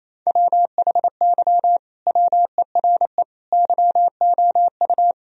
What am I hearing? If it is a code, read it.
Morse code, heard as a keyed tone, about 28 wpm, 700 Hz. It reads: WHY WERE YOU
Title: Sentences from Top 1000 Words - Review Part 4 - 28wpm